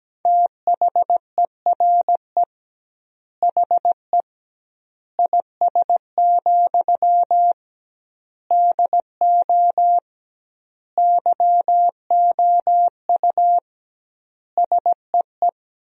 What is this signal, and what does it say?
Morse code, heard as a keyed tone, about 17 wpm, 700 Hz. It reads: THERE HE IS, DO YOU SEE
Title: 2x Word-Spacing - Sentences from Top 100 Words - 17wpm